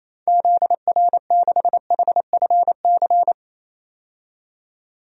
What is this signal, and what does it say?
Morse code, heard as a keyed tone, about 28 wpm, 700 Hz. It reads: ZR6HFC
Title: DXCC-100 Call Signs - 6 Characters - 28wpm